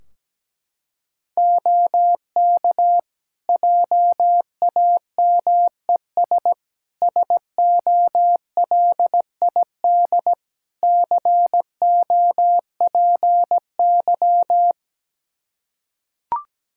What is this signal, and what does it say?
Morse code, heard as a keyed tone, about 17 wpm, 700 Hz. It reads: OK JAMES SOLID COPY
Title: Practice QSO - Exchange 3 - More Info - 17wpm